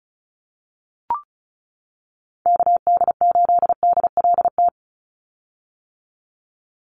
Morse code, heard as a keyed tone, about 34 wpm, 700 Hz. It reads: KD8DLT
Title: USA Call Signs 35wpm